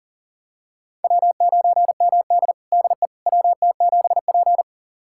Morse code, heard as a keyed tone, about 40 wpm, 700 Hz. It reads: W9MD DE WT7P
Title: USA Callsign DE Callsign - 4 Characters - 40wpm